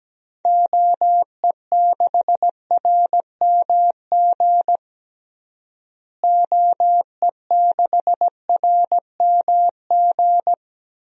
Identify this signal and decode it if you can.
Morse code, heard as a keyed tone, about 17 wpm, 700 Hz. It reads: OE6RMG OE6RMG
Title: DXCC-100 Call Signs - 6 Characters - Repeated 2x - 17wpm